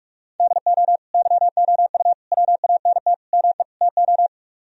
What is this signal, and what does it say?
Morse code, heard as a keyed tone, about 50 wpm, 700 Hz. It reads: DO YOU WANT ME TO